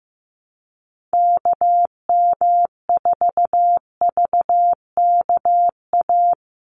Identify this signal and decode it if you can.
Morse code, heard as a keyed tone, about 15 wpm, 700 Hz. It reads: KM4VKA
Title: USA Call Signs 15wpm